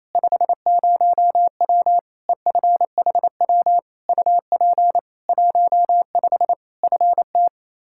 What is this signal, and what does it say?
Morse code, heard as a keyed tone, about 28 wpm, 700 Hz. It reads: 50W EFHW UP 15 FT
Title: K1USN SST QSO Practice 28wpm